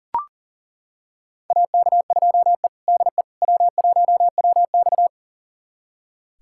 Morse code, heard as a keyed tone, about 38 wpm, 700 Hz. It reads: AK2E DE W1WX
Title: USA Callsign DE Callsign - 4 Characters - 40wpm